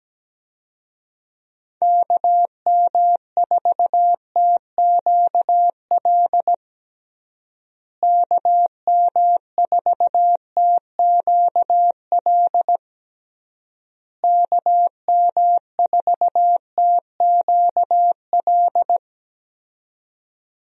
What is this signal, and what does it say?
Morse code, heard as a keyed tone, about 17 wpm, 700 Hz. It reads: KM4TQL KM4TQL KM4TQL
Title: USA Call Signs - 6 Characters - Repeated 3x - 17wpm